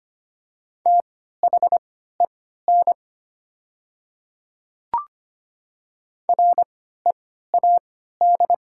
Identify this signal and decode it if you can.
Morse code, heard as a keyed tone, about 25 wpm, spaced as for 14 wpm, 700 Hz. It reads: THEN READ
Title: 3x Character Spacing - Top 200 Words - 25wpm